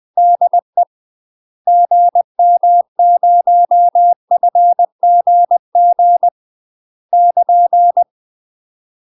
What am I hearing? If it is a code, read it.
Morse code, heard as a keyed tone, about 20 wpm, 700 Hz. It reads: DE GM0FGG <KN>
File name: UK MASTER QSOs TRAINING / 20WPM / UK MASTER QSOs 0049